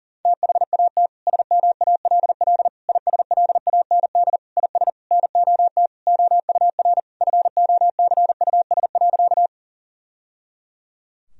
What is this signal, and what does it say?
Morse code, heard as a keyed tone, about 37 wpm, 700 Hz. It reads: THAT SMALL ISLAND IS NOT OUR FOCUS.
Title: Sentences from Top 1000 Words - Review Part 4 - 40wpm